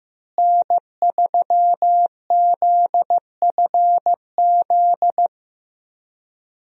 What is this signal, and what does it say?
Morse code, heard as a keyed tone, about 15 wpm, 700 Hz. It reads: N3ZFZ
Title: USA Call Signs - 5 Characters - Repeated 2x - 15wpm